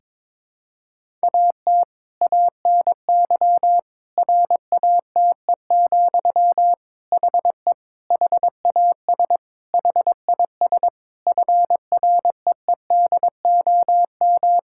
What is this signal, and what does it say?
Morse code, heard as a keyed tone, about 22 wpm, 700 Hz. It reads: AT ANY RATE, HE HAS HIS FREEDOM
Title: Sentences from Top 1000 Words - Review Part 5 - 22wpm